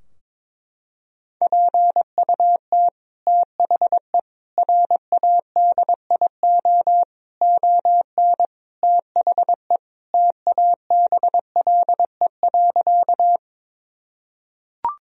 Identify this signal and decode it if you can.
Morse code, heard as a keyed tone, about 22 wpm, 700 Hz. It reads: PUT THE RADIO ON THE TABLE.
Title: Sentences from Top 1000 Words - Review Part 3 - 22wpm